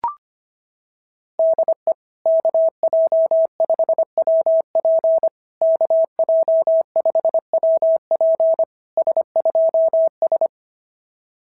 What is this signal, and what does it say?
Morse code, heard as a keyed tone, about 23 wpm, 650 Hz. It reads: DE KJ5WP KJ5WP S2S
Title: SOTA QSO Practice 25wpm